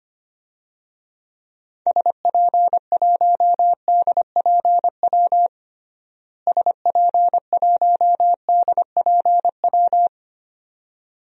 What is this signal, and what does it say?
Morse code, heard as a keyed tone, about 25 wpm, 700 Hz. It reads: SP1DPW SP1DPW
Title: DXCC-100 Call Signs - 6 Characters - Repeated 2x - 25wpm